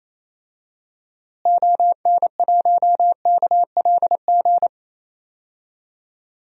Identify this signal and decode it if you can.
Morse code, heard as a keyed tone, about 28 wpm, 700 Hz. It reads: ON1KLG
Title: DXCC-100 Call Signs - 6 Characters - Repeated 3x - 28wpm